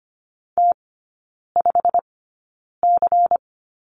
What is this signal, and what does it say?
Morse code, heard as a keyed tone, about 25 wpm, spaced as for 9 wpm, 700 Hz. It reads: T5C
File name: Lesson 17e - Letter u - Three Characters 25wpm